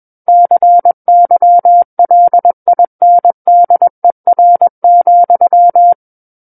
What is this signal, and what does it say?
Morse code, heard as a keyed tone, about 21 wpm, 700 Hz. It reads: CYLINDER,
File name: Book One / 12-Chapter 12 - 21 WPM